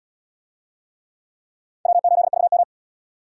Morse code, heard as a keyed tone, about 76 wpm, 700 Hz. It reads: W8VM